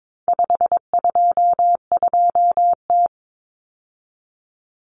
Morse code, heard as a keyed tone, about 22 wpm, 700 Hz. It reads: 522T
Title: Common Cut Numbers - 4-Digit 22wpm